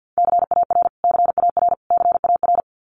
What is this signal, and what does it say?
Morse code, heard as a keyed tone, about 50 wpm, 700 Hz. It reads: CAR CAR CAR
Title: Top 500 Words Repeated 50wpm